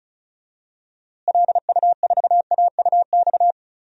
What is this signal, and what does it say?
Morse code, heard as a keyed tone, about 36 wpm, 700 Hz. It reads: LU4AUX
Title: DXCC-100 Call Signs - 6 Characters - Repeated 2x - 35wpm